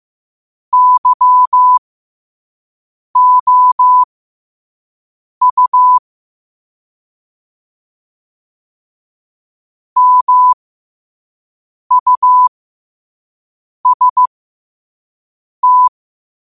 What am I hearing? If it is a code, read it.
Morse code, heard as a keyed tone, about 15 wpm, spaced as for 5 wpm, 1000 Hz. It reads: YOU MUST
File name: 5wpm